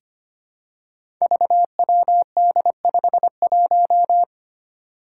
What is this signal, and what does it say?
Morse code, heard as a keyed tone, about 25 wpm, 700 Hz. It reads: VWD51
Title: Five Challenging Letter-Number Combinations 25wpm